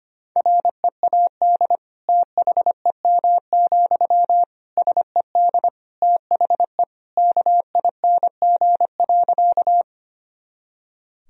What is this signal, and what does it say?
Morse code, heard as a keyed tone, about 25 wpm, 700 Hz. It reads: READ THEM, SED THE KING.
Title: Sentences from Top 400 Words and Abbreviations 25wpm